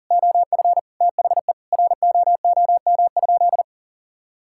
Morse code, heard as a keyed tone, about 40 wpm, 700 Hz. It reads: OF THE ROOM?
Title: Sentences from Top 1000 Words - Review Part 6 - 40wpm